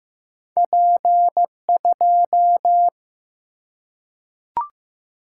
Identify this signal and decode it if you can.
Morse code, heard as a keyed tone, about 15 wpm, 700 Hz. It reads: P2
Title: Single-Letter Single-Number 15wpm